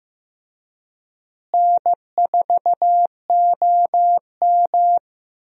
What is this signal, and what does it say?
Morse code, heard as a keyed tone, about 15 wpm, 700 Hz. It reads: N4OM
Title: USA Call Signs - 4 Characters - Repeated 2x - 15wpm